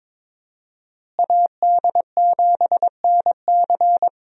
Morse code, heard as a keyed tone, about 22 wpm, 700 Hz. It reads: AD7NC
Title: USA Call Signs - 5 Characters - Repeated 3x - 22wpm